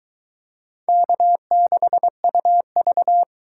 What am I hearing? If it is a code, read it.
Morse code, heard as a keyed tone, about 23 wpm, 700 Hz. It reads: K6UV